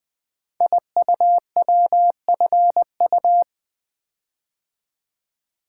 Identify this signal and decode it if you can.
Morse code, heard as a keyed tone, about 20 wpm, 700 Hz. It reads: IUWFU
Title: Code Groups - 5 Letters - 20wpm